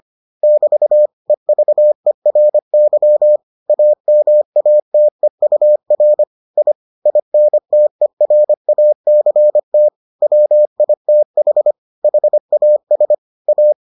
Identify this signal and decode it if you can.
Morse code, heard as a keyed tone, about 25 wpm, 600 Hz. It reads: <BT> EVERY AMATEUR I INTERACT WITH HAS A